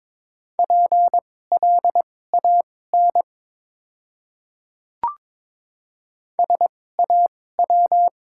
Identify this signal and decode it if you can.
Morse code, heard as a keyed tone, about 22 wpm, spaced as for 16 wpm, 700 Hz. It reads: PLAN SAW
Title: 2x Character Spacing - Top 500 Words - 22wpm